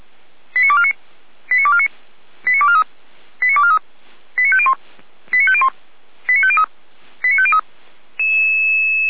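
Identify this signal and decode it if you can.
Five-tone selective call (ZVEI), heard as quick runs of five tones, tones of about 70 ms. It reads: 89249 89249 89243 89243 89501 89501 89503 89503
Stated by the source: ZVEI five-tone selective call received on an SDR.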